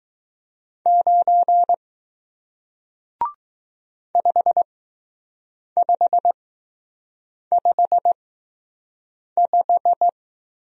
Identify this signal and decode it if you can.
Morse code, heard as a keyed tone, about 20 wpm, spaced as for 6 wpm, 700 Hz. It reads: 9 5555